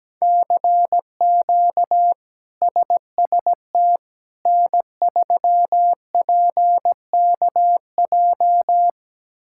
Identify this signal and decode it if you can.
Morse code, heard as a keyed tone, about 17 wpm, 700 Hz. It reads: CQ SST N3PKJ
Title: K1USN SST QSO Practice - 17wpm